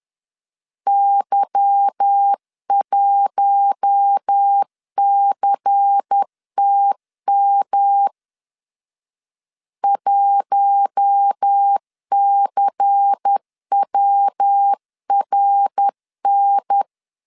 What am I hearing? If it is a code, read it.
Morse code, heard as a keyed tone, about 10 wpm, 800 Hz. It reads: Y1CTM 1CWRN
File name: LEZIONI / 8 YJ45 / 8 Y 10wpm 2615